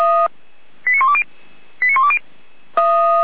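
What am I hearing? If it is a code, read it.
Five-tone selective call (ZVEI), heard as quick runs of five tones, tones of about 70 ms. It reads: 89120 89120
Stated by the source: ZVEI five-tone selective call 89094, received on SDR.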